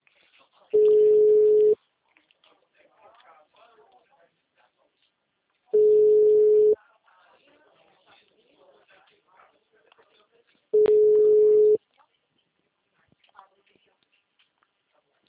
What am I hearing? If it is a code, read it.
Ringback tone (Europe, 425 Hz), with speech.